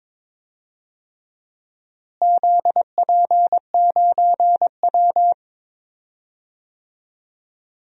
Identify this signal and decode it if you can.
Morse code, heard as a keyed tone, about 22 wpm, 700 Hz. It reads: ZP9W
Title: DXCC-100 Call Signs - 4 Characters - Repeated 3x - 22wpm